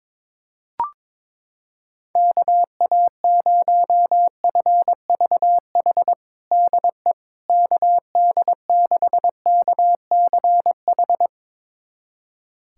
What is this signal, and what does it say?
Morse code, heard as a keyed tone, about 23 wpm, 700 Hz. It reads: KA0FVH DE KD6KCH
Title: USA Callsign DE Callsign - 6 Characters - 22wpm